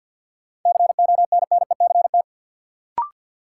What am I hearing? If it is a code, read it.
Morse code, heard as a keyed tone, about 50 wpm, 700 Hz. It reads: CONNECT
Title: Sentences from Top 1500 Words - New Word Drill - 50wpm